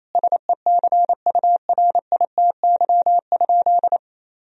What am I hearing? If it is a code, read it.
Morse code, heard as a keyed tone, about 28 wpm, 700 Hz. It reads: SECURITY?
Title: Sentences from Top 1000 Words - Review Part 4 - 28wpm